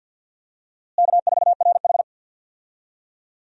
Morse code, heard as a keyed tone, about 48 wpm, 700 Hz. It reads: K4RH
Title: USA Call Signs - 4 Characters - Repeated 2x - 50wpm